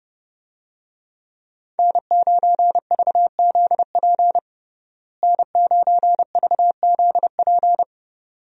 Morse code, heard as a keyed tone, about 31 wpm, 700 Hz. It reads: N9VZP N9VZP
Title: USA Call Signs - 5 Characters - Repeated 2x - 30wpm